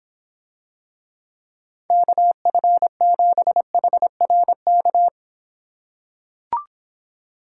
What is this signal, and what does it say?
Morse code, heard as a keyed tone, about 26 wpm, 700 Hz. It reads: KF7HRK